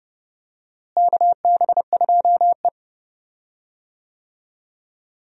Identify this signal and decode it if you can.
Morse code, heard as a keyed tone, about 32 wpm, 700 Hz. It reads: KB2E
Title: USA Call Signs - 4 Characters - Repeated 3x - 30wpm